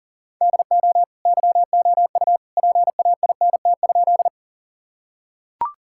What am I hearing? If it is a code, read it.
Morse code, heard as a keyed tone, about 40 wpm, 700 Hz. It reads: DO YOU PAINT?
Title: Sentences from Top 1000 Words - Review Part 6 - 40wpm